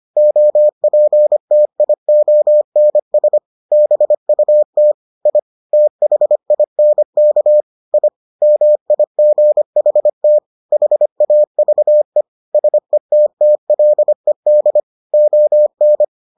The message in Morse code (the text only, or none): OPTIONS BUT I THINK I MIGHT HAVE SETTLED ON